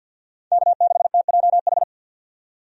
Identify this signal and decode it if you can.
Morse code, heard as a keyed tone, about 48 wpm, 700 Hz. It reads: K6TJH
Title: USA Call Signs - 5 Characters - 50wpm